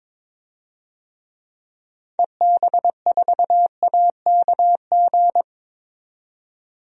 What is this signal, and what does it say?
Morse code, heard as a keyed tone, about 22 wpm, 700 Hz. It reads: EB4AKG
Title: DXCC-100 Call Signs - 6 Characters - 22wpm